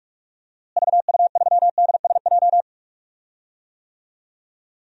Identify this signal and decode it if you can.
Morse code, heard as a keyed tone, about 46 wpm, 700 Hz. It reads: VU3DSJ
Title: DXCC-100 Call Signs - 6 Characters - Repeated 3x - 45wpm